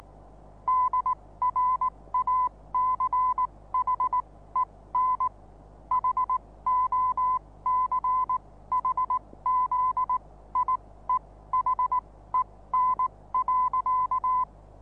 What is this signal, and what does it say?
Morse code, heard as a keyed tone, about 19 wpm, spaced as for 15 wpm, 1010 Hz. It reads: DRACHEN HOCHZIEHEN.
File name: repos / Morsekurs / DJ4UF Morsen26